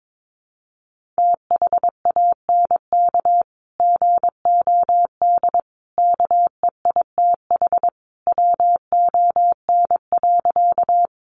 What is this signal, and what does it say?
Morse code, heard as a keyed tone, about 22 wpm, 700 Hz. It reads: THANK GOD KEITH WON.
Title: A Sound Of Thunder 22wpm